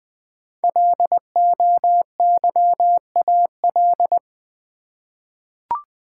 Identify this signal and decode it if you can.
Morse code, heard as a keyed tone, about 20 wpm, 700 Hz. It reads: LOYAL
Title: 5 Letter Words - First 3 Letters Given - 20wpm